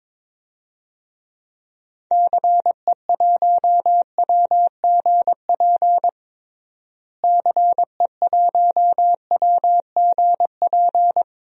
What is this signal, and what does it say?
Morse code, heard as a keyed tone, about 22 wpm, 700 Hz. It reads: CE1WGP CE1WGP
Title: DXCC-100 Call Signs - 6 Characters - Repeated 3x - 22wpm